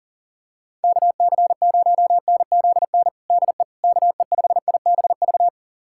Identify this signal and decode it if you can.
Morse code, heard as a keyed tone, about 40 wpm, 700 Hz. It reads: KC0NZN DE KE5IBV
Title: USA Callsign DE Callsign 40wpm